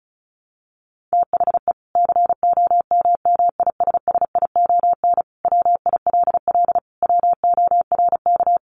Morse code, heard as a keyed tone, about 34 wpm, 700 Hz. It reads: THE COMMISSION WILL WORK
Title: Sentences from Top 1400 Words 35wpm